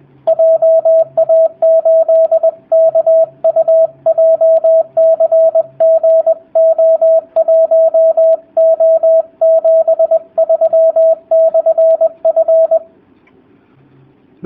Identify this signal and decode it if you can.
Morse code, heard as a keyed tone, about 20 wpm, 650 Hz, with radio noise. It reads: JA8KU JCGO1O73/F
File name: jh8cbh / 2019cw / sp20- w